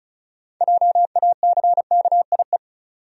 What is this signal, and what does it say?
Morse code, heard as a keyed tone, about 34 wpm, 700 Hz. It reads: JACKIE
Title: USA Common Names - 35wpm